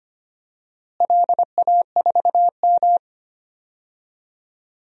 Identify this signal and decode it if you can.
Morse code, heard as a keyed tone, about 25 wpm, 700 Hz. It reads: LA4M
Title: DXCC-100 Call Signs - 4 Characters - Repeated 3x - 25wpm